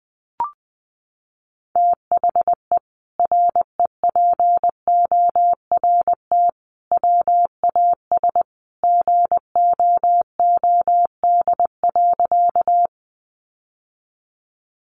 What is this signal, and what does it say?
Morse code, heard as a keyed tone, about 19 wpm, 700 Hz. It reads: THE REPORT WAS GOOD.
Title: Sentences from Top 900 Words 20wpm